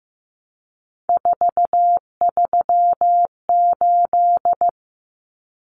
Morse code, heard as a keyed tone, about 15 wpm, 700 Hz. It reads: 438
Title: Numbers - 3-Digit 15wpm